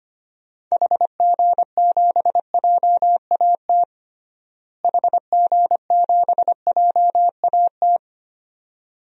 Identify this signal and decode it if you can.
Morse code, heard as a keyed tone, about 25 wpm, 700 Hz. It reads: HG7JAT HG7JAT
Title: DXCC-100 Call Signs - 6 Characters - Repeated 2x - 25wpm